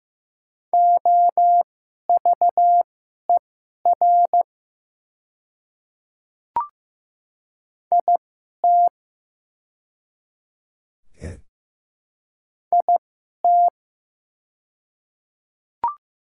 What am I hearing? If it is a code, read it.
Morse code, heard as a keyed tone, about 15 wpm, spaced as for 11 wpm, 700 Hz. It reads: OVER IT IT
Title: 2x Character Spacing - Top 100 Words - 15wpm